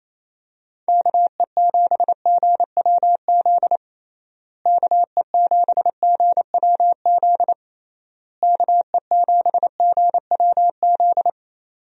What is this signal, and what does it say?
Morse code, heard as a keyed tone, about 28 wpm, 700 Hz. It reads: KE7GWZ KE7GWZ KE7GWZ
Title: USA Call Signs - 6 Characters - Repeated 3x - 28wpm